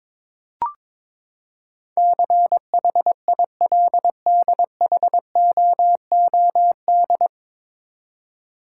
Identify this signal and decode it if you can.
Morse code, heard as a keyed tone, about 22 wpm, 700 Hz. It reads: CHILDHOOD
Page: Morse Code Ninja practice 22wpm